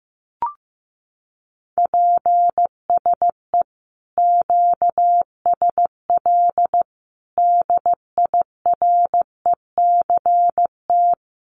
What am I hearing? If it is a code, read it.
Morse code, heard as a keyed tone, about 15 wpm, 700 Hz. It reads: PSE QSL DIRECT
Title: Common QSO Phrases 15wpm